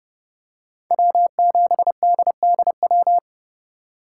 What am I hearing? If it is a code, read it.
Morse code, heard as a keyed tone, about 28 wpm, 700 Hz. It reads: W7DDW